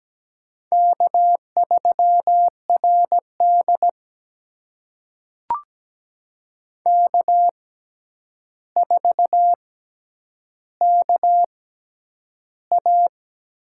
Morse code, heard as a keyed tone, about 17 wpm, spaced as for 9 wpm, 700 Hz. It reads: K3RD K4KA
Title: USA Call Signs - 4 Characters - 6x Character Spacing - 17wpm